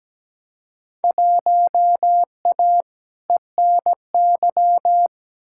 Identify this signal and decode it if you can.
Morse code, heard as a keyed tone, about 17 wpm, 700 Hz. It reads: 1A ENY
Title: Field Day Exchanges Encore - 17wpm